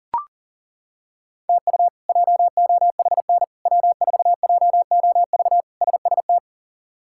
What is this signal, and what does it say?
Morse code, heard as a keyed tone, about 37 wpm, 700 Hz. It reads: TU JOHN W4JOV SST
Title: K1USN SST QSO Practice - 40wpm